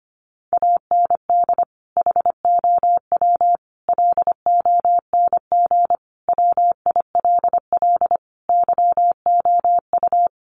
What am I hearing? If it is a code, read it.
Morse code, heard as a keyed tone, about 25 wpm, 700 Hz. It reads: AND HOW LONG WILL YOU
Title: Sentences from Top 1200 Words 25wpm